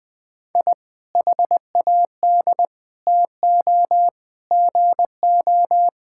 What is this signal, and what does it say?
Morse code, heard as a keyed tone, about 20 wpm, 700 Hz. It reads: I HAD TO GO
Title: Sentences with 3-Word Phrases - Spoken Beginning - 20wpm